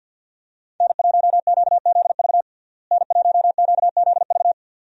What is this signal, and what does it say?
Morse code, heard as a keyed tone, about 48 wpm, 700 Hz. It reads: N1QZV N1QZV
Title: USA Call Signs - 5 Characters - Repeated 2x - 50wpm